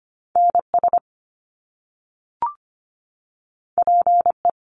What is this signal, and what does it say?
Morse code, heard as a keyed tone, about 25 wpm, 700 Hz. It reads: NS PE
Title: Canadian Provinces and Territories 25wpm